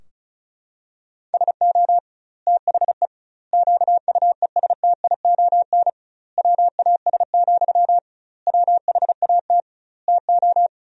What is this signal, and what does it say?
Morse code, heard as a keyed tone, about 36 wpm, 700 Hz. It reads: SO THE QUESTION WAS, WHAT TO
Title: 2x Word-Spacing - Sentences from Top 400 Words - 35wpm